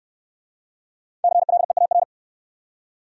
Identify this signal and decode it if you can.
Morse code, heard as a keyed tone, about 62 wpm, 700 Hz. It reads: CLEAR